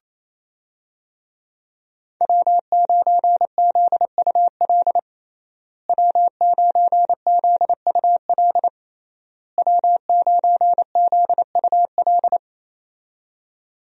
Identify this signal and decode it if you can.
Morse code, heard as a keyed tone, about 28 wpm, 700 Hz. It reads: W9ZUL W9ZUL W9ZUL
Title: USA Call Signs - 5 Characters - Repeated 3x - 28wpm